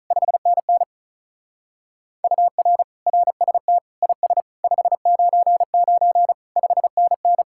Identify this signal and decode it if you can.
Morse code, heard as a keyed tone, about 43 wpm, 700 Hz. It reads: 5NN UR RST IS 599 5NN
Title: Speed-Racing Common QSO Phrases - 35wpm